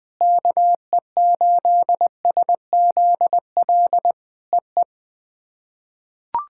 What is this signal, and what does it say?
Morse code, heard as a keyed tone, about 21 wpm, 700 Hz. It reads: KE8SZL EE
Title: POTA QSO Practice - 20wpm